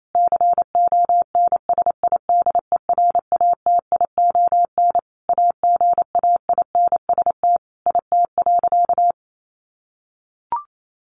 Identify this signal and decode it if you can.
Morse code, heard as a keyed tone, about 28 wpm, 700 Hz. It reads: CONSIDERATION AGAINST IT.
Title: Sentences from Top 1500 Words 28wpm